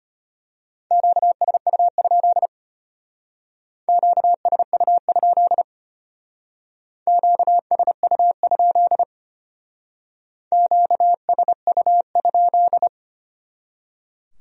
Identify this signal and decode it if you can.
Morse code, heard as a keyed tone, about 31 wpm, 700 Hz. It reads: QSU? QSU? QSU? QSU?
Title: Speed-Racing Q-Codes - 25wpm